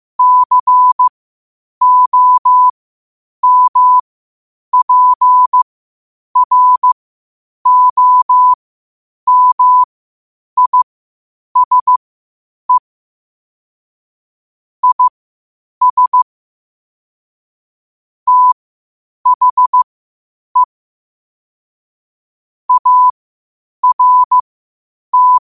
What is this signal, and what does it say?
Morse code, heard as a keyed tone, about 15 wpm, spaced as for 8 wpm, 1000 Hz. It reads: COMPROMISE IS THE ART
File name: qotdfiles / 8wpm